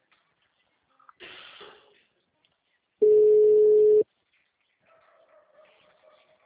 Ringback tone (Europe, 425 Hz).